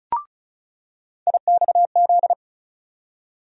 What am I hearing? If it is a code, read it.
Morse code, heard as a keyed tone, about 35 wpm, 700 Hz. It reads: IXZ
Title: USA Call Sign Postfixes - 3 Letters 35wpm